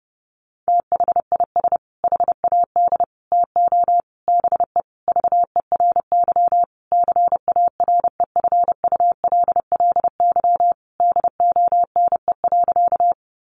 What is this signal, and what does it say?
Morse code, heard as a keyed tone, about 30 wpm, 700 Hz. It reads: THIS HAD TO BE VERY CAREFULLY DONE.
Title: Sentences from Top 1300 Words 30wpm